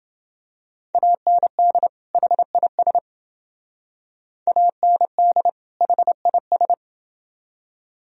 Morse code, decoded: AND HIS AND HIS

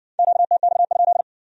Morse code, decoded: QTX?

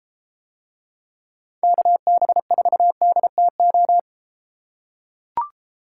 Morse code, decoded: KB4DTO

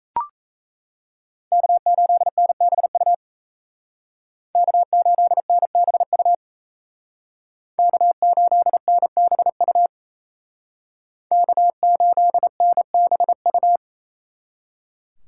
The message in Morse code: K8NBU K8NBU K8NBU K8NBU

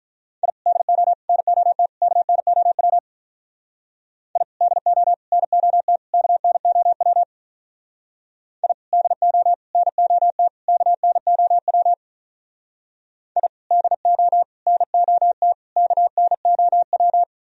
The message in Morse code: I DO NOT KNOW I DO NOT KNOW I DO NOT KNOW I DO NOT KNOW

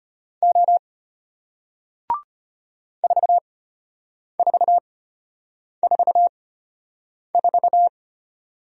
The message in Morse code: O 4444

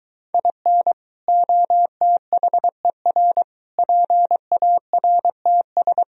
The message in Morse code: IN OTHER PARTS

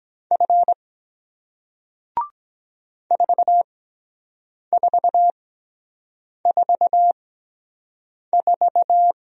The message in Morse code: F 4444

